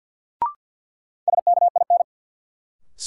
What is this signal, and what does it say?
Morse code, heard as a keyed tone, about 47 wpm, 700 Hz. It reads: SKIN
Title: Sentences from Top 1300 Words - New Word Drill - 50wpm